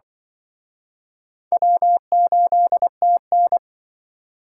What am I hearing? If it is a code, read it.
Morse code, heard as a keyed tone, about 24 wpm, 700 Hz. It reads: W8TN
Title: Speed Racing - USA 1x2 Call Signs 20wpm